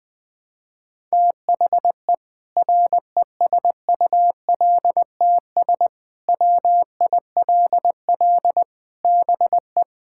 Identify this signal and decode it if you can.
Morse code, heard as a keyed tone, about 20 wpm, 700 Hz. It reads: THE RESULTS WILL BE
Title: Sentences with 7-Word Phrases - 20wpm